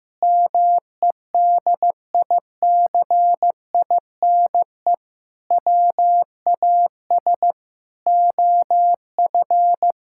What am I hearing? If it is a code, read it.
Morse code, heard as a keyed tone, about 16 wpm, 700 Hz. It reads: MEDICINE WAS OF
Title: Sentences from Top 1000 Words - Review Part 6 - 15wpm